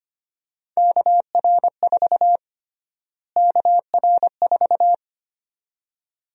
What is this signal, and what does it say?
Morse code, heard as a keyed tone, about 25 wpm, 700 Hz. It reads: KR4 KR4
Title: USA Call Sign Prefixes - Repeated 2x - 25wpm